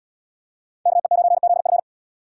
Morse code, heard as a keyed tone, about 75 wpm, 700 Hz. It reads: KE9ZF